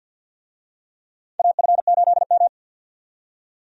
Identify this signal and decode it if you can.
Morse code, heard as a keyed tone, about 52 wpm, 700 Hz. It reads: AF8M
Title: USA Call Signs - 4 Characters - 50wpm